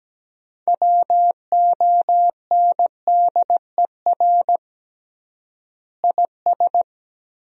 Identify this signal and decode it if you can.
Morse code, heard as a keyed tone, about 17 wpm, 700 Hz. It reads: WONDER IS
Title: 3x Word-Spacing - Sentences from Top 500 Words - 17wpm